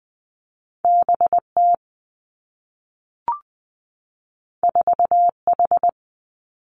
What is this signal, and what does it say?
Morse code, heard as a keyed tone, about 19 wpm, 700 Hz. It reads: B T 4 H
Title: Two Challenging Letter-Number Combinations 20wpm